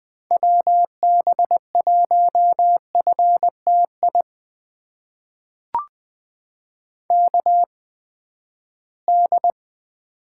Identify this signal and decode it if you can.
Morse code, heard as a keyed tone, about 20 wpm, spaced as for 14 wpm, 700 Hz. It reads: WB1FTI KD
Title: USA Call Signs - 6 Characters - 8x Character Spacing - 20wpm